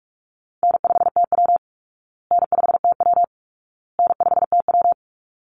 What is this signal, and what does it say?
Morse code, heard as a keyed tone, about 44 wpm, 700 Hz. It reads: N5TW N5TW N5TW
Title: Easy Call Signs Repeated 3x - 45wpm